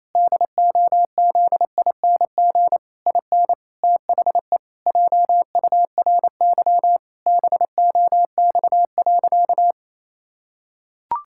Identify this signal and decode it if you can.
Morse code, heard as a keyed tone, about 27 wpm, 700 Hz. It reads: DOZING IN THE JURY BOX.
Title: Pangram Challenge v2 - 28wpm